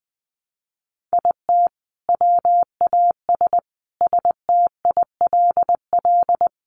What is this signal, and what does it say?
Morse code, heard as a keyed tone, about 20 wpm, 700 Hz. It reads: IT WAS STILL
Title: Sentences from Top 900 Words 20wpm